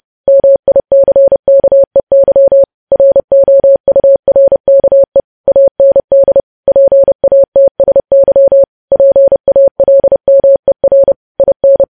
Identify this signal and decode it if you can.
Morse code, heard as a keyed tone, about 30 wpm, 550 Hz. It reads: MICKEY ROURKE AND PATSY PALMER IN
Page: news headlines in morse code at 30 wpm